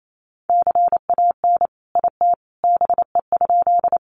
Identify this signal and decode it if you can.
Morse code, heard as a keyed tone, about 28 wpm, 700 Hz. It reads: CAN IT BE?
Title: Sentences from Top 100 Words 28wpm